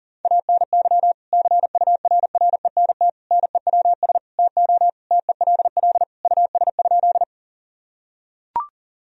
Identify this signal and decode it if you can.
Morse code, heard as a keyed tone, about 40 wpm, 700 Hz. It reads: ANY CURRENT NEWS TO TELL US?
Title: Sentences from Top 1000 Words - Review Part 4 - 40wpm